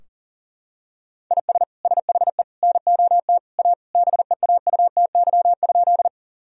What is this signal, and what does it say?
Morse code, heard as a keyed tone, about 40 wpm, 700 Hz. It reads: IS SHE NOT A BEAUTY?
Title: Sentences from Top 1000 Words - Review Part 2 - 40wpm